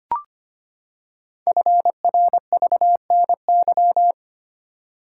Morse code, heard as a keyed tone, about 25 wpm, 700 Hz. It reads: FRVNY
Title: Code Groups - 5 Letters - 25wpm